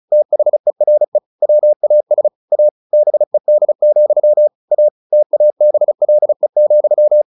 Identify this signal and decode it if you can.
Morse code, heard as a keyed tone, about 34 wpm, 600 Hz. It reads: THERE WAS A BED, A TABLE,
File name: the old man and the sea / the old man and the sea 1 35wpm